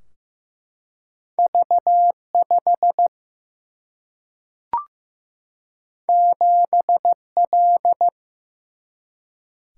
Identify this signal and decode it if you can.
Morse code, heard as a keyed tone, about 15 wpm, 700 Hz. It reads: V 5 7 L
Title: Single-Letter Single-Number 15wpm